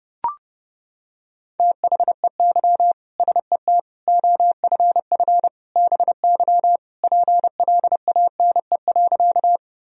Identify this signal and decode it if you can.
Morse code, heard as a keyed tone, about 32 wpm, 700 Hz. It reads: THEY SET OFF BY PLANE.
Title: Sentences from Top 1000 Words - Review Part 3 - 30wpm